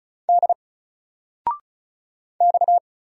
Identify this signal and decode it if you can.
Morse code, heard as a keyed tone, about 35 wpm, 700 Hz. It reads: D X
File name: Single-Letter-Number-35wpm